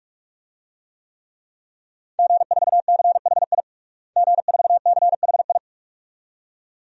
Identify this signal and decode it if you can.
Morse code, heard as a keyed tone, about 46 wpm, 700 Hz. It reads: G4CHI G4CHI
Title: DXCC-100 Call Signs - 5 Characters - Repeated 2x - 45wpm